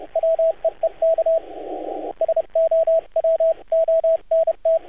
Morse code, heard as a keyed tone, about 30 wpm, 650 Hz, with radio noise. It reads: WEEK SO WONT
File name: archived / CRSnet